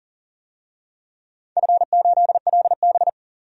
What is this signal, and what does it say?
Morse code, heard as a keyed tone, about 40 wpm, 700 Hz. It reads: F8LB
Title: DXCC-100 Call Signs - 4 Characters - 40wpm